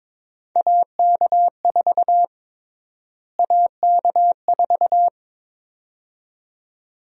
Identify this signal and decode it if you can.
Morse code, heard as a keyed tone, about 22 wpm, 700 Hz. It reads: AK4 AK4
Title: USA Call Sign Prefixes - Repeated 2x - 22wpm